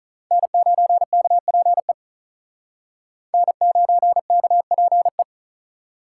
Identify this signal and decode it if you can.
Morse code, heard as a keyed tone, about 44 wpm, 700 Hz. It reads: N9KPE N9KPE